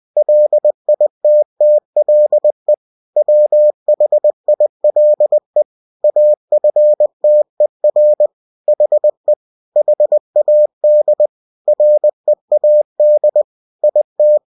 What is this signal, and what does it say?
Morse code, heard as a keyed tone, about 20 wpm, 600 Hz. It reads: LITTLE WHILE AFTER HE HAD READ IT
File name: cw-books / winnie-the-pooh / winnie-20wpm-page-0064